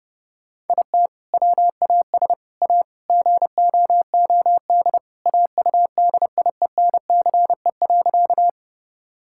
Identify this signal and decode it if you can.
Morse code, heard as a keyed tone, about 30 wpm, 700 Hz. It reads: IT WAS A GOOD AUDIENCE.
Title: Sentences from Top 1000 Words - Review Part 5 - 30wpm